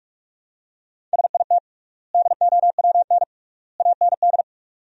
Morse code, heard as a keyed tone, about 41 wpm, 700 Hz. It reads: SIT DOWN AND
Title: 3x Word-Spacing - Sentences from Top 500 Words - 45wpm